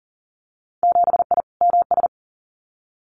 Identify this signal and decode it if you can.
Morse code, heard as a keyed tone, about 40 wpm, 700 Hz. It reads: 7I MS
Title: Winter Field Day Exchanges Encore 40wpm